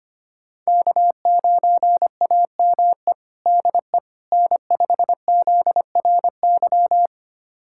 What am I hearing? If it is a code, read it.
Morse code, heard as a keyed tone, about 25 wpm, 700 Hz. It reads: K9AME DE N5ZRY
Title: USA Callsign DE Callsign - 5 Characters - 25wpm